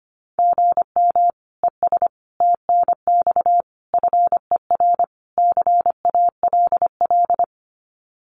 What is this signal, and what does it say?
Morse code, heard as a keyed tone, about 25 wpm, 700 Hz. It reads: GM ES TNX FER CALL
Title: Common QSO Phrases Encore 25wpm